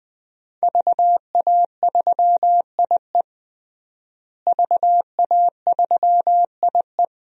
Morse code, heard as a keyed tone, about 20 wpm, 700 Hz. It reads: VA3IE VA3IE
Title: DXCC-100 Call Signs - 5 Characters - Repeated 2x - 20wpm